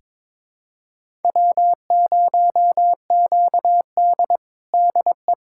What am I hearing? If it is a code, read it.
Morse code, heard as a keyed tone, about 22 wpm, 700 Hz. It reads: W0QD DE
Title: USA Callsign DE Callsign - 4 Characters - 22wpm